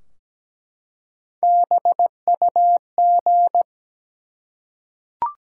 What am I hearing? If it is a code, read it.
Morse code, heard as a keyed tone, about 17 wpm, 700 Hz. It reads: BUG
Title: Three Letter Words - 17wpm